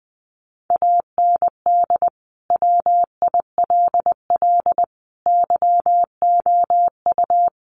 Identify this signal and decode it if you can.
Morse code, heard as a keyed tone, about 20 wpm, 700 Hz. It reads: AND WILL YOU
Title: Sentences from Top 1500 Words 20wpm